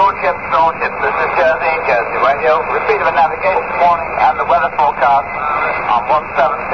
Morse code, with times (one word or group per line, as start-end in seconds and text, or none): none